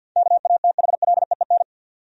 Morse code, heard as a keyed tone, about 53 wpm, 700 Hz. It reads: KATHLEEN
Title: USA Common Names - 50wpm